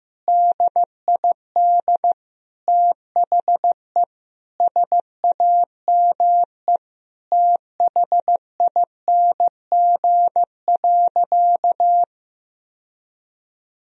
Morse code, heard as a keyed tone, about 15 wpm, 700 Hz. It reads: DID THE SAME THING.
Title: Sentences from Top 1000 Words - Review Part 1 - 15wpm